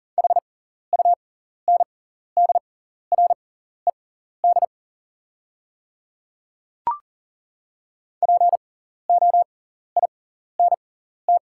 Morse code, heard as a keyed tone, about 40 wpm, spaced as for 14 wpm, 690 Hz. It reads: HUNDRED POINT
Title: 6x Character Spacing - Top 500 Words - 40wpm